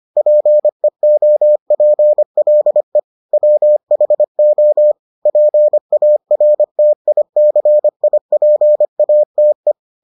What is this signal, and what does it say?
Morse code, heard as a keyed tone, about 25 wpm, 600 Hz. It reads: PEOPLE WHO PARTICIPATE